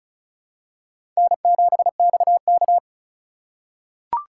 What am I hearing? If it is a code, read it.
Morse code, heard as a keyed tone, about 34 wpm, 700 Hz. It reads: N7XK
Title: USA Call Signs - 4x Character Spacing - 35wpm